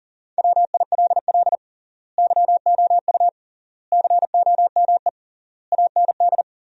Morse code, heard as a keyed tone, about 37 wpm, 700 Hz. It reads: WILL YOU COME AND
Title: 3x Word-Spacing - Sentences from Top 100 Words - 40wpm